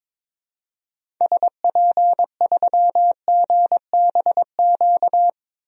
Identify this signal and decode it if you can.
Morse code, heard as a keyed tone, about 22 wpm, 700 Hz. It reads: SP3GBQ
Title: DXCC-100 Call Signs - 6 Characters - 22wpm